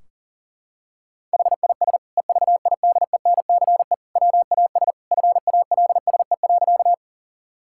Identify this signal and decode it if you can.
Morse code, heard as a keyed tone, about 40 wpm, 700 Hz. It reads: HIS EVIDENCE WAS FALSE.
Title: Sentences from Top 1600 Words 40wpm